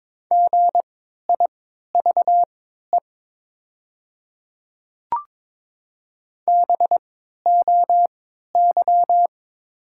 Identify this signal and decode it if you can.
Morse code, heard as a keyed tone, about 22 wpm, spaced as for 12 wpm, 700 Hz. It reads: GIVE BOY
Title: 3x Character Spacing - Top 200 Words - 22wpm